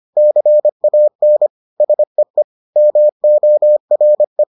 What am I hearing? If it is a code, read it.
Morse code, heard as a keyed tone, about 25 wpm, 600 Hz. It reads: CAN SEE MORE